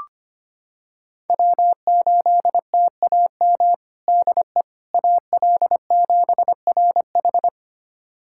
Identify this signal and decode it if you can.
Morse code, heard as a keyed tone, about 25 wpm, 700 Hz. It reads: W8TAM DE AL7RH
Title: USA Callsign DE Callsign - 5 Characters - 25wpm